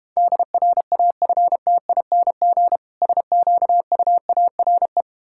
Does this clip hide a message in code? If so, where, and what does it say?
Yes: Morse code from 0.2 to 5.0 s, heard as a keyed tone, about 32 wpm, 700 Hz. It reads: DRAFTING SQUARE